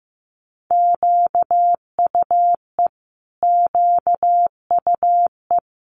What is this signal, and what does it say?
Morse code, heard as a keyed tone, about 15 wpm, 700 Hz. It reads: QUE QUE
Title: Multi-letter Phonemes Repeated 3x 15wpm